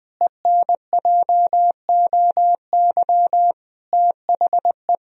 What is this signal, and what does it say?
Morse code, heard as a keyed tone, about 21 wpm, 700 Hz. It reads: ENJOY THE
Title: Sentences from Top 1000 Words - Review Part 4 - 20wpm